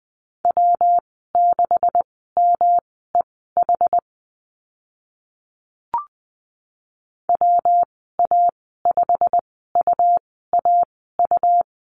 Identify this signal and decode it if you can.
Morse code, heard as a keyed tone, about 20 wpm, spaced as for 14 wpm, 700 Hz. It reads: W6MEH WA5UAU
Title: 161 Lesson 26g - Letter g - Call Signs 20wpm